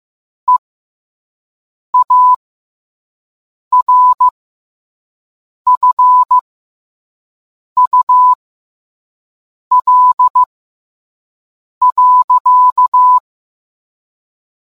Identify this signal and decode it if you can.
Morse code, heard as a keyed tone, about 15 wpm, spaced as for 5 wpm, 1000 Hz. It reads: EARFUL.